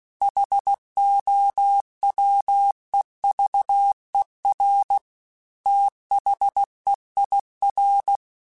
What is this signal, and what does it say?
Morse code, heard as a keyed tone, about 16 wpm, 800 Hz. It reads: HOWEVER THEIR
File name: T then H Runs / T then H Runs.mp3 / T then H Runs @15